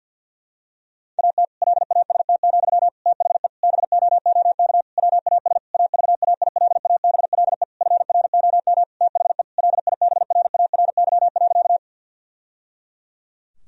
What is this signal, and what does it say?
Morse code, heard as a keyed tone, about 50 wpm, 700 Hz. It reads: AT LAST, THE BOOK WAS AVAILABLE FROM THE LIBRARY.
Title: Sentences from Top 1000 Words - Review Part 3 - 50wpm